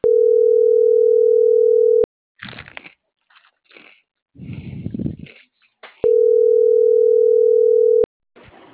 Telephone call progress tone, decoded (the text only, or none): ringback tone